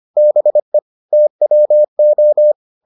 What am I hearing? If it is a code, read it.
Morse code, heard as a keyed tone, about 25 wpm, 600 Hz. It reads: BE TWO